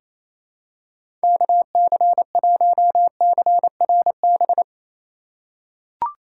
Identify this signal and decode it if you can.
Morse code, heard as a keyed tone, about 28 wpm, 700 Hz. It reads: KC1CRB
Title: USA Call Signs - 6 Characters - 3x Character Spacing - 28wpm